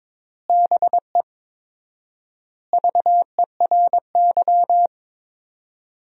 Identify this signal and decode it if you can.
Morse code, heard as a keyed tone, about 22 wpm, 700 Hz. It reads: BE VERY
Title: 4x Word-Spacing - Sentences from Top 200 Words - 22wpm